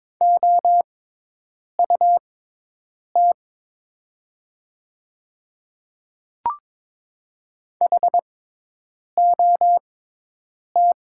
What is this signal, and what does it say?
Morse code, heard as a keyed tone, about 22 wpm, spaced as for 8 wpm, 700 Hz. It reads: OUT HOT
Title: 6x Character Spacing - Top 100 Words - 22wpm